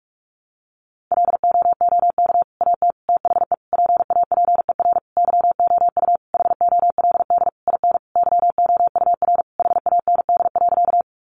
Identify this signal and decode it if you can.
Morse code, heard as a keyed tone, about 44 wpm, 700 Hz. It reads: LOOK AT THE PAPER YOU HOLD IN YOUR HAND.
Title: Sentences from Top 400 Words 45wpm